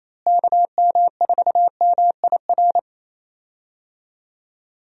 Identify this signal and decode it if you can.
Morse code, heard as a keyed tone, about 27 wpm, 700 Hz. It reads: KM4MIR
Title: USA Call Signs - 6 Characters - Repeated 2x - 28wpm